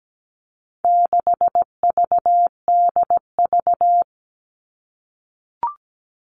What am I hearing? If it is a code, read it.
Morse code, heard as a keyed tone, about 17 wpm, 700 Hz. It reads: 6VDV
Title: All Cut Numbers - 4-Digit 17wpm